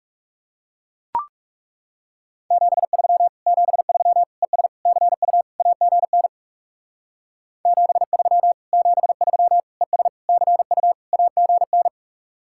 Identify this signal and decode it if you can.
Morse code, heard as a keyed tone, about 36 wpm, 700 Hz. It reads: 73 73 ES CU AGN 73 73 ES CU AGN
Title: Speed-Racing Common QSO Phrases Encore - 30wpm